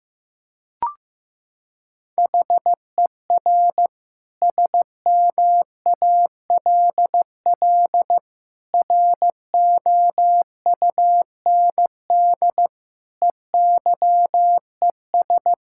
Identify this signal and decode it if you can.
Morse code, heard as a keyed tone, about 14 wpm, 700 Hz. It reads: HER SMALL ROUND EYES
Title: Sentences from Top 1000 Words - Review Part 6 - 15wpm